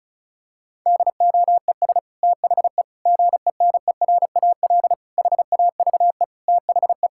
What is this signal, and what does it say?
Morse code, heard as a keyed tone, about 36 wpm, 700 Hz. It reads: DOES THE GENERAL HAVE THE
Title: Sentences from Top 1000 Words - Review Part 6 - 35wpm